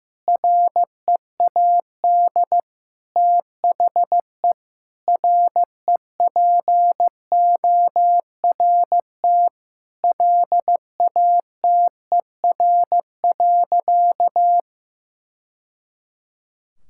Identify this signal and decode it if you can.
Morse code, heard as a keyed tone, about 15 wpm, 700 Hz. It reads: READ THE REPORT LATER.
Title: Sentences from Top 1000 Words - Review Part 6 - 15wpm